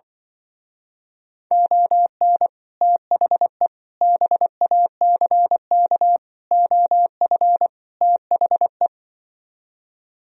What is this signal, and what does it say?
Morse code, heard as a keyed tone, about 24 wpm, 700 Hz. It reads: ON THE BACK OF THE
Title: Speed-Racing - Sets of 5 Words 20wpm